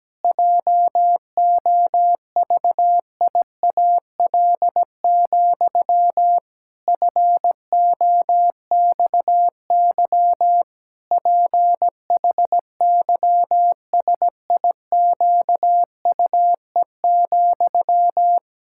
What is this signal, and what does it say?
Morse code, heard as a keyed tone, about 17 wpm, 700 Hz. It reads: JOVIAL, FOXY PHYSIQUE,
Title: Pangram Challenge v2 - 17wpm